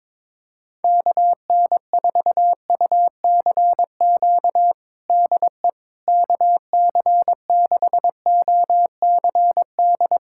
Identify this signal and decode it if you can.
Morse code, heard as a keyed tone, about 22 wpm, 700 Hz. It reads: KN4UCQ DE KC6OCD
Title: USA Callsign DE Callsign - 6 Characters - 22wpm